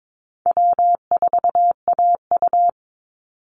Morse code, heard as a keyed tone, about 22 wpm, 700 Hz. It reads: W4AU
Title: Easy Call Signs Repeated 2x - 22wpm